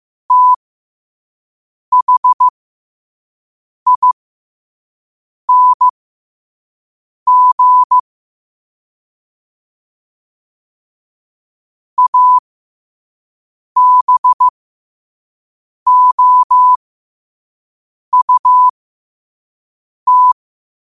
Morse code, heard as a keyed tone, about 15 wpm, spaced as for 5 wpm, 1000 Hz. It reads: THING ABOUT
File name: 5wpm